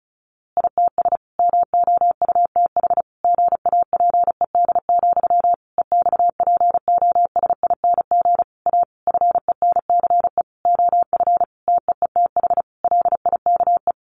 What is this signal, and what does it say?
Morse code, heard as a keyed tone, about 34 wpm, 700 Hz. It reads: ITS MOUTH GAPED, EXPOSING A FENCE OF TEETH LIKE
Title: A Sound Of Thunder 35wpm